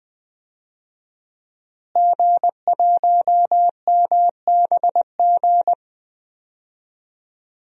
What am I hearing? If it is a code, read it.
Morse code, heard as a keyed tone, about 20 wpm, 700 Hz. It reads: G1MBG
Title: DXCC-100 Call Signs - 5 Characters - Repeated 3x - 20wpm